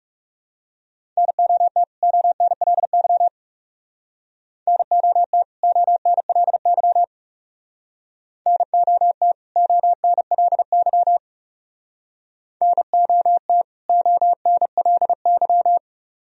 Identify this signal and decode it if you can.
Morse code, heard as a keyed tone, about 37 wpm, 700 Hz. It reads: NOT ONLY NOT ONLY NOT ONLY NOT ONLY
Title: Speed-Racing - Sets of 2 Words 30wpm